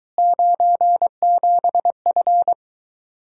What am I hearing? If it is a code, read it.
Morse code, heard as a keyed tone, about 23 wpm, 700 Hz. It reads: 97F